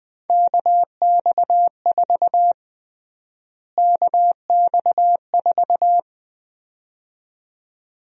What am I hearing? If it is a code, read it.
Morse code, heard as a keyed tone, about 20 wpm, 700 Hz. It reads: KX4 KX4
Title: USA Call Sign Prefixes - Repeated 2x - 20wpm